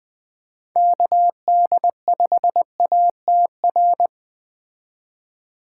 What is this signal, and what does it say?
Morse code, heard as a keyed tone, about 21 wpm, 700 Hz. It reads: KD5ATR
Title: Speed Racing - USA 2x3 Call Signs 17wpm